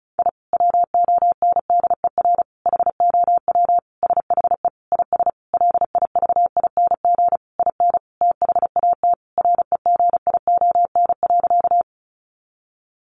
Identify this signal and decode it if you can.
Morse code, heard as a keyed tone, about 34 wpm, 700 Hz. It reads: I WONDER HOW SHE IS LIVING IN THAT REGION.
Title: Sentences from Top 1500 Words 35wpm